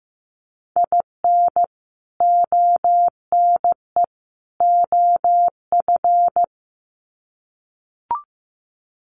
Morse code, heard as a keyed tone, about 15 wpm, 700 Hz. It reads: IN ONE OF
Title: Sets of 3 Words 15wpm